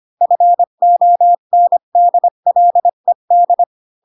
Morse code, heard as a keyed tone, about 25 wpm, 700 Hz. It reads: FONDLED
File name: Pinocchio / Pinocchio ch7 20 25 ik1hge